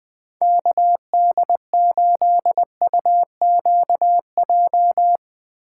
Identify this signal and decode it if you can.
Morse code, heard as a keyed tone, about 20 wpm, 700 Hz. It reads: KD8UQJ
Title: Speed Racing - USA 2x3 Call Signs 17wpm